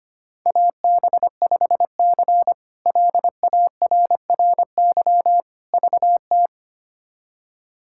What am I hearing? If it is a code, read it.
Morse code, heard as a keyed tone, about 25 wpm, 700 Hz. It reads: AB5C LARRY VT
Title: K1USN SST QSO Practice - 25wpm